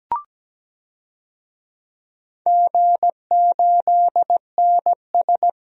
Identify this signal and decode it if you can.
Morse code, heard as a keyed tone, about 17 wpm, 700 Hz. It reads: G8NS
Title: DXCC-100 Call Signs - 4 Characters - 17wpm